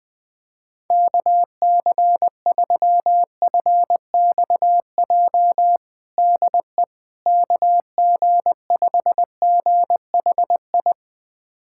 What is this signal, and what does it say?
Morse code, heard as a keyed tone, about 20 wpm, 690 Hz. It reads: KC3FXJ DE KG5GHI
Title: USA Callsign DE Callsign 20wpm